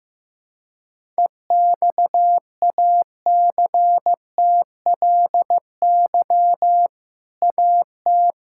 Morse code, heard as a keyed tone, about 15 wpm, 700 Hz. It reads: EXACTLY AT
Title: Sentences from Top 1000 Words - Review Part 6 - 15wpm